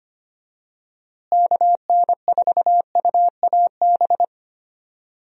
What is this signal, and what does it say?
Morse code, heard as a keyed tone, about 25 wpm, 700 Hz. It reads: KN4UAB
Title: USA Call Signs - 6 Characters - Repeated 2x - 25wpm